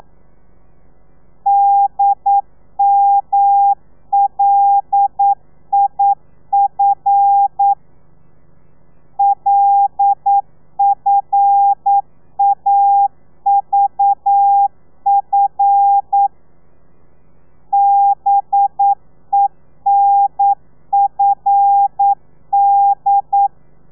Morse code, heard as a keyed tone, about 9 wpm, 800 Hz. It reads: DMLIF LFAVF BENFD
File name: telegrafia / i2tfj / Lez5 40